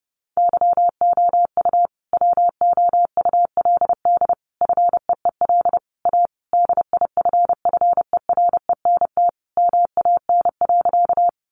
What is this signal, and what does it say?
Morse code, heard as a keyed tone, about 30 wpm, 700 Hz. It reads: YOU WOULD FEEL A DIFFERENT MAN.
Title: Sentences from Top 600 Words 30wpm